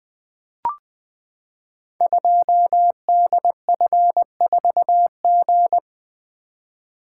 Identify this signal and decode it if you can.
Morse code, heard as a keyed tone, about 20 wpm, 700 Hz. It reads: 2DF4G
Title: Code Groups - 5 Letter:Numbers - 20wpm